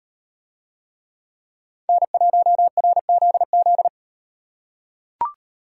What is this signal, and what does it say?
Morse code, heard as a keyed tone, about 36 wpm, 700 Hz. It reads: N1RZZ